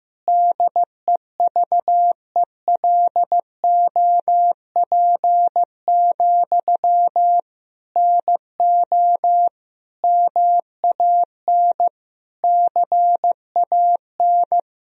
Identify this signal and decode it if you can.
Morse code, heard as a keyed tone, about 15 wpm, 700 Hz. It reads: DEVELOP, NO MAN CAN
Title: Sentences from Top 1000 Words - Review Part 2 - 15wpm